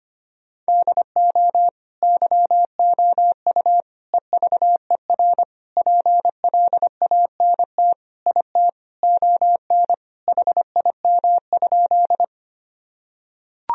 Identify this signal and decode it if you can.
Morse code, heard as a keyed tone, about 25 wpm, 700 Hz. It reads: DO YOU EVER PLANT IT ON HIM?
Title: Sentences from Top 1000 Words - Review Part 1 - 25wpm